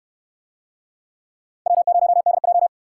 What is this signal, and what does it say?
Morse code, heard as a keyed tone, about 62 wpm, 700 Hz. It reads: W0DJ